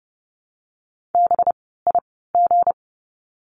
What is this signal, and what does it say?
Morse code, heard as a keyed tone, about 30 wpm, spaced as for 17 wpm, 700 Hz. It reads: BIG
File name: course / 30:10wpm / 237 Lesson 38f - Letter q - Words 30wpm